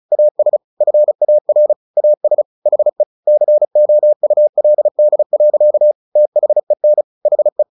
Morse code, heard as a keyed tone, about 34 wpm, 600 Hz. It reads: AS FAR AS HE COULD. THEN HE
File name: the old man and the sea 2 35wpm